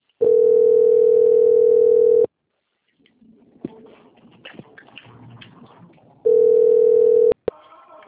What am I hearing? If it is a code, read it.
Ringback tone (North America), 440 + 480 Hz.
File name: out- - - - - .37469